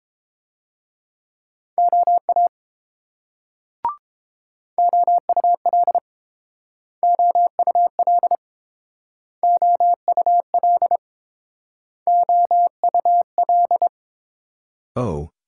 Morse code, heard as a keyed tone, about 29 wpm, 700 Hz. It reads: OA OUL OUL OUL OUL